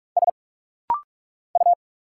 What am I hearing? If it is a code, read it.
Morse code, heard as a keyed tone, about 46 wpm, 700 Hz. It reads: S U
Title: Single Letter-Number Alt - Rapid-Fire 45wpm